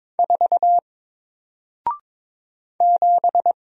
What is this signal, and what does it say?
Morse code, heard as a keyed tone, about 22 wpm, 700 Hz. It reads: 4 7
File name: Single-Letter-Number-22wpm